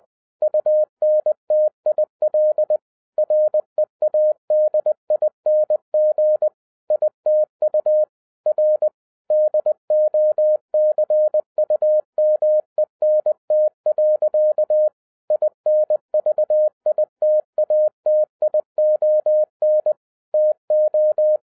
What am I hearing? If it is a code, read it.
Morse code, heard as a keyed tone, about 20 wpm, 610 Hz. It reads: UNTIL READING ITU R DOCUMENT. INVITATION TO